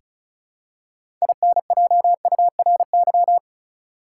Morse code, heard as a keyed tone, about 36 wpm, 700 Hz. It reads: INJURY
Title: Six Letter Words - 35wpm